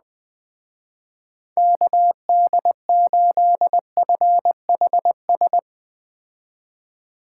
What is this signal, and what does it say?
Morse code, heard as a keyed tone, about 20 wpm, 700 Hz. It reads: KD8FHS
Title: Speed Racing - USA Call Signs 17wpm